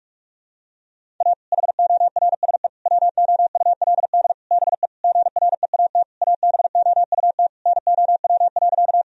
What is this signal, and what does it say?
Morse code, heard as a keyed tone, about 46 wpm, 700 Hz. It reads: A HORSE WOULD BE GREAT ABOUT NOW.
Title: Sentences from Top 1000 Words - Review Part 1 - 45wpm